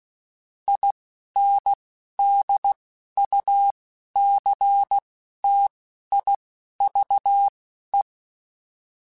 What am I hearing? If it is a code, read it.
Morse code, heard as a keyed tone, about 16 wpm, spaced as for 11 wpm, 800 Hz. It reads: INDUCTIVE
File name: Technical Subjects / Technical Subjects-mp3s / XL @10